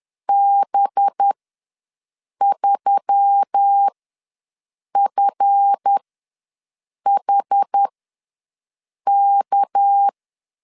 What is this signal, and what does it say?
Morse code, heard as a keyed tone, about 10 wpm, spaced as for 6 wpm, 800 Hz. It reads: B3FHK
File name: LEZIONI / 7 GF9Z / 7 GF ws 10wpm 3918